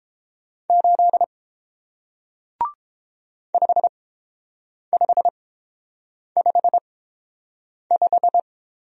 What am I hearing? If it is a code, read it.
Morse code, heard as a keyed tone, about 30 wpm, spaced as for 7 wpm, 700 Hz. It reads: 8 5555